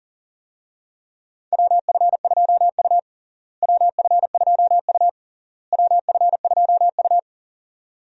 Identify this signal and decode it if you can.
Morse code, heard as a keyed tone, about 40 wpm, 700 Hz. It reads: WF2U WF2U WF2U
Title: USA Call Signs - 4 Characters - Repeated 3x - 40wpm